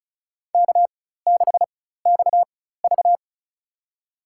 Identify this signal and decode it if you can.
Morse code, heard as a keyed tone, about 34 wpm, spaced as for 16 wpm, 700 Hz. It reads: K6XV
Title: USA Call Signs - 4 Characters - 4x Character Spacing - 35wpm